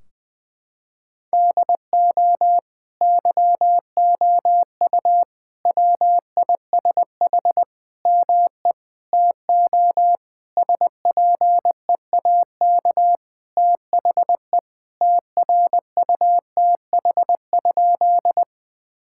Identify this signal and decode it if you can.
Morse code, heard as a keyed tone, about 20 wpm, 700 Hz. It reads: DO YOU WISH ME TO SPEAK THE TRUTH?
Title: Sentences from Top 1600 Words 20wpm